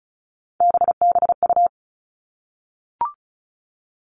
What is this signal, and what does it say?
Morse code, heard as a keyed tone, about 36 wpm, 700 Hz. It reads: BBU
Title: Three Challenging Letter-Number Combinations 35wpm